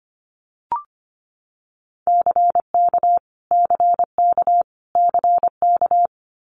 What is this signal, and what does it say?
Morse code, heard as a keyed tone, about 25 wpm, 700 Hz. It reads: CK CK CK
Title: Multi-letter Phonemes Repeated 3x 25wpm